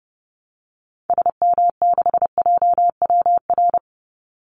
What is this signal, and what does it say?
Morse code, heard as a keyed tone, about 30 wpm, 700 Hz. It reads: SM6JWR
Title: Call Signs Repeated 3x - 30wpm